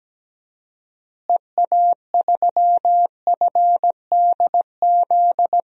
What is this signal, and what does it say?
Morse code, heard as a keyed tone, about 17 wpm, 700 Hz. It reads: EA3FDZ
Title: DXCC-100 Call Signs - 6 Characters - Repeated 3x - 17wpm